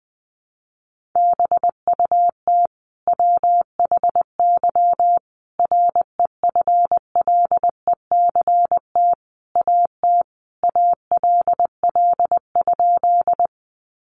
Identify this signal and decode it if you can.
Morse code, heard as a keyed tone, about 20 wpm, 700 Hz. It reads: BUT WHY REFLECT AT ALL?
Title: Sentences from Top 1500 Words 20wpm